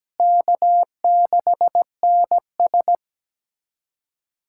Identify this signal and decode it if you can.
Morse code, heard as a keyed tone, about 17 wpm, 700 Hz. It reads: K6NS
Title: USA Callsign DE Callsign - 4 Characters - 17wpm